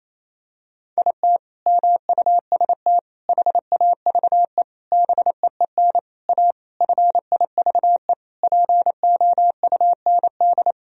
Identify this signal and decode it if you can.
Morse code, heard as a keyed tone, about 28 wpm, 700 Hz. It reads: IT MUST HAVE BEEN A FIVE POUND
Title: Sentences from Top 1000 Words - Review Part 2 - 28wpm